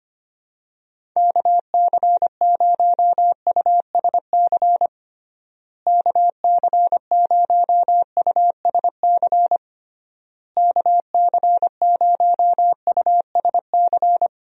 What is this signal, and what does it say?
Morse code, heard as a keyed tone, about 25 wpm, 700 Hz. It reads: KC0USC KC0USC KC0USC
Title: USA Call Signs - 6 Characters - Repeated 3x - 25wpm